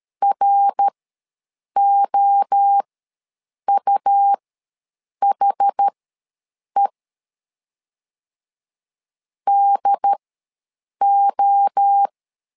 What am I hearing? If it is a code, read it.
Morse code, heard as a keyed tone, about 13 wpm, spaced as for 7 wpm, 800 Hz. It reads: ROUHE DO
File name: LEZIONI / 3 MWRO / 3 MWRO ws 12wpm 1914